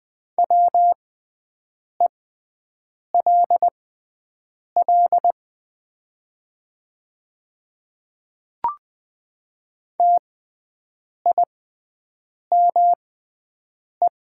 Morse code, heard as a keyed tone, about 20 wpm, spaced as for 7 wpm, 700 Hz. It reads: WELL TIME
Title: 6x Character Spacing - Top 300 Words - 20wpm